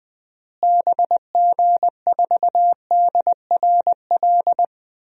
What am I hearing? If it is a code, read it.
Morse code, heard as a keyed tone, about 20 wpm, 700 Hz. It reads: BG4DRL
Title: DXCC-100 Call Signs - 6 Characters - Repeated 3x - 20wpm